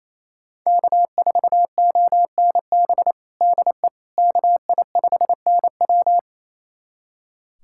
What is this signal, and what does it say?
Morse code, heard as a keyed tone, about 28 wpm, 700 Hz. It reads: K4ONB DE KI5NW
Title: USA Callsign DE Callsign 28wpm